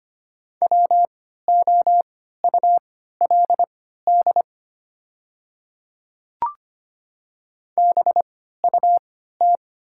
Morse code, heard as a keyed tone, about 25 wpm, spaced as for 14 wpm, 700 Hz. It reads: WOULD BUT
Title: 3x Character Spacing - Top 200 Words - 25wpm